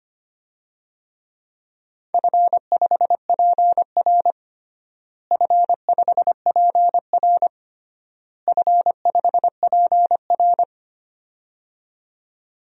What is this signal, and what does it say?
Morse code, heard as a keyed tone, about 25 wpm, 700 Hz. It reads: F5PR F5PR F5PR
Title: DXCC-100 Call Signs - 4 Characters - Repeated 3x - 25wpm